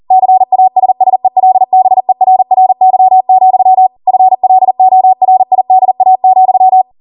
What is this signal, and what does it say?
Morse code, heard as a keyed tone, about 40 wpm, 750 Hz. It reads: CASSELBERRY, FLORIDA,